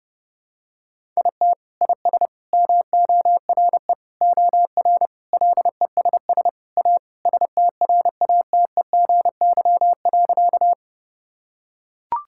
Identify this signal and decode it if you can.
Morse code, heard as a keyed tone, about 32 wpm, 700 Hz. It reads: IT IS MORE OR LESS A STRATEGY.
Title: Sentences from Top 1000 Words - Review Part 5 - 30wpm